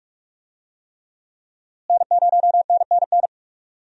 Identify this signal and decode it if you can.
Morse code, heard as a keyed tone, about 46 wpm, 700 Hz. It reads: N0NNN
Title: USA Call Signs - 5 Characters - Repeated 3x - 45wpm